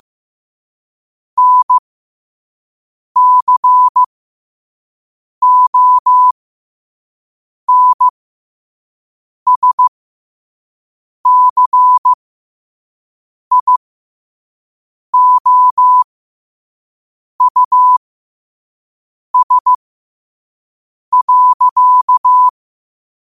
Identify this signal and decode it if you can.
Morse code, heard as a keyed tone, about 15 wpm, spaced as for 5 wpm, 1000 Hz. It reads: NCONSCIOUS.
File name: qotdfiles / 5wpm